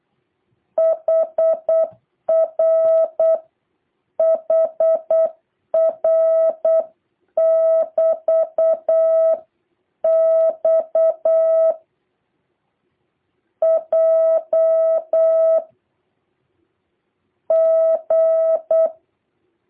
Morse code, heard as a keyed tone, about 8 wpm, 650 Hz. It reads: HR HR <BT> X J G
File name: jh8cbh / 2023cw